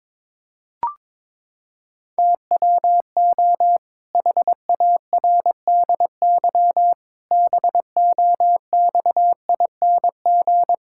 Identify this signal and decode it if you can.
Morse code, heard as a keyed tone, about 22 wpm, 700 Hz. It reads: TWO HARDY BOXING
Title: Pangram Challenge v2 - 22wpm